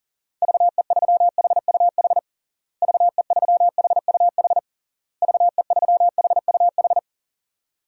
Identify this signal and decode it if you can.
Morse code, heard as a keyed tone, about 40 wpm, 700 Hz. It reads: VE3HUH VE3HUH VE3HUH
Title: DXCC-100 Call Signs - 6 Characters - Repeated 3x - 40wpm